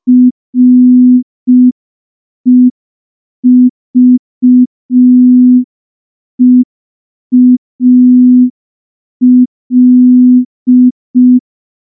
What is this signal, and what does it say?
Morse code, heard as a keyed tone, about 5 wpm, 250 Hz. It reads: REVEAL
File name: morse / morse-training-05WPM